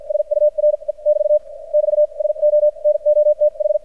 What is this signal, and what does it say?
Morse code, heard as a keyed tone, about 52 wpm, spaced as for 43 wpm, 600 Hz. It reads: SURE <BT> <BT> SO NOTH
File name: archived / CRSnet